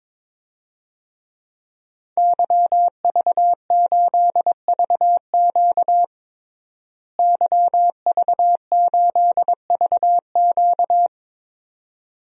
Morse code, heard as a keyed tone, about 22 wpm, 700 Hz. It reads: YV8VQ YV8VQ
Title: DXCC-100 Call Signs - 5 Characters - Repeated 2x - 22wpm